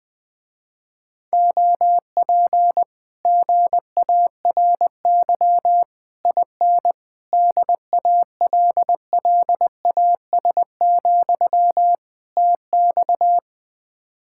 Morse code, heard as a keyed tone, about 20 wpm, 700 Hz. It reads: OP GARY IN DALLAS, TX
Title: Speed-Racing Common QSO Phrases Encore - 17wpm